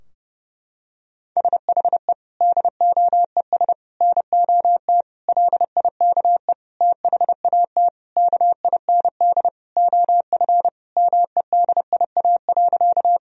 Morse code, heard as a keyed tone, about 30 wpm, 700 Hz. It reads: SHE DOES NOT LIKE THAT KIND OF MEDIA.
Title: Sentences from Top 1000 Words - Review Part 4 - 30wpm